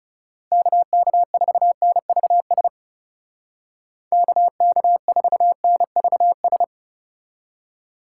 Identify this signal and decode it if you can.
Morse code, heard as a keyed tone, about 38 wpm, 700 Hz. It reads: KK4NVS KK4NVS